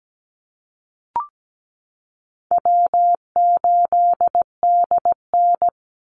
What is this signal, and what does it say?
Morse code, heard as a keyed tone, about 17 wpm, 700 Hz. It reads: W8DN
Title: Easy Call Signs Repeated 2x - 17wpm